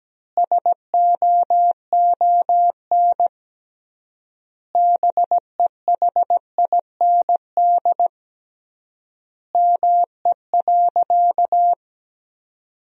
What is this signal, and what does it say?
Morse code, heard as a keyed tone, about 17 wpm, 700 Hz. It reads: SOON BEHIND ME.
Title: 3x Word-Spacing - Sentences from Top 500 Words - 17wpm